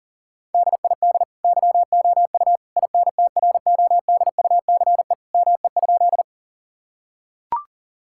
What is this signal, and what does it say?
Morse code, heard as a keyed tone, about 40 wpm, 700 Hz. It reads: DID YOU INTRODUCE ME?
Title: Sentences from Top 1600 Words 40wpm